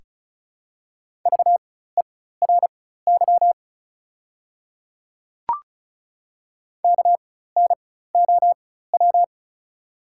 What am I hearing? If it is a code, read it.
Morse code, heard as a keyed tone, about 33 wpm, spaced as for 16 wpm, 690 Hz. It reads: VERY KNOW
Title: 4x Character Spacing - Top 300 Words - 35wpm